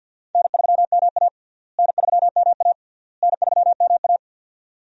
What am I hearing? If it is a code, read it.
Morse code, heard as a keyed tone, about 46 wpm, 700 Hz. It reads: N3MA N3MA N3MA
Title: USA Call Signs - 4 Characters - Repeated 3x - 50wpm